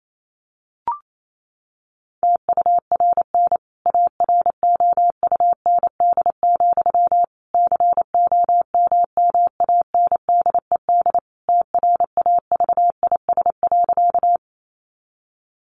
Morse code, heard as a keyed tone, about 28 wpm, 700 Hz. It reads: TURN AROUND, COMMANDED TRAVIS.
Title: A Sound Of Thunder 28wpm